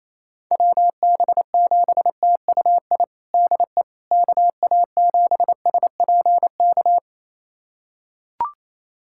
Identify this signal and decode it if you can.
Morse code, heard as a keyed tone, about 28 wpm, 700 Hz. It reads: WB7TUI DE KA7SPK
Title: USA Callsign DE Callsign 28wpm